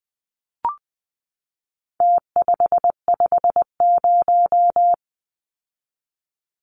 Morse code, heard as a keyed tone, about 20 wpm, 700 Hz. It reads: T550
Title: Numbers - Contest Style - 4-Digit 20wpm